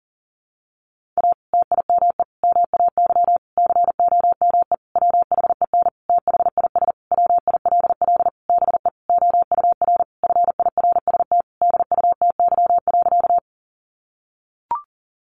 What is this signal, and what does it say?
Morse code, heard as a keyed tone, about 40 wpm, 700 Hz. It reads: A TIME MAY COME WHEN THIS WILL BE OUR FIRST DUTY.
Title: Sentences from Top 1500 Words 40wpm